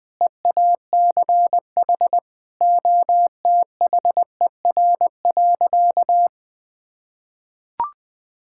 Morse code, heard as a keyed tone, about 20 wpm, 700 Hz. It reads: EACH OTHER.
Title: Sentences with 2-Word Phrases - Spoken Beginning - 20wpm